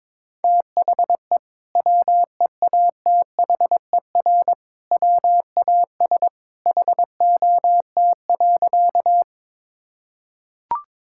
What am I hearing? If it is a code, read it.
Morse code, heard as a keyed tone, about 22 wpm, 700 Hz. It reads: THE WEATHER WAS HOT.
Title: Sentences from Top 1000 Words - Review Part 5 - 22wpm